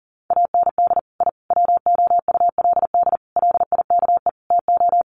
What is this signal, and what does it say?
Morse code, heard as a keyed tone, about 40 wpm, 700 Hz. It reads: AND I WOULD LIKE TO
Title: Sets of 5 Words Encore 40wpm